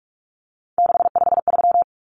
Morse code, heard as a keyed tone, about 44 wpm, 700 Hz. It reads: 653
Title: Numbers - 3-Digit 45wpm